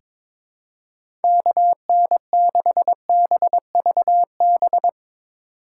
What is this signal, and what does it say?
Morse code, heard as a keyed tone, about 22 wpm, 700 Hz. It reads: KN6BVB
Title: USA Call Signs - 6 Characters - 3x Character Spacing - 22wpm